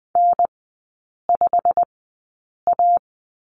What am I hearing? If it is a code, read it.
Morse code, heard as a keyed tone, about 20 wpm, spaced as for 8 wpm, 700 Hz. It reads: N5A
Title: 087 Lesson 15e - Number 5 - Three Characters 20wpm